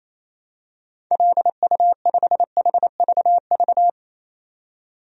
Morse code, heard as a keyed tone, about 28 wpm, 700 Hz. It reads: LU5HVV
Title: DXCC-100 Call Signs - 6 Characters - 28wpm